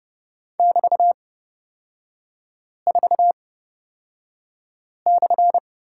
Morse code, heard as a keyed tone, about 29 wpm, spaced as for 5 wpm, 700 Hz. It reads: <BT>4/
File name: Lesson 36e - BT - Three Characters 30wpm